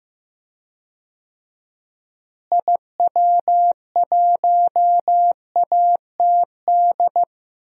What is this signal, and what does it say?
Morse code, heard as a keyed tone, about 15 wpm, 700 Hz. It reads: IW1ATD
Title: DXCC-100 Call Signs - 6 Characters - Repeated 3x - 15wpm